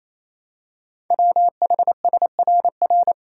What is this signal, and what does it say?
Morse code, heard as a keyed tone, about 28 wpm, 700 Hz. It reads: WHSRR
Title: Code Groups - 5 Letters - 28wpm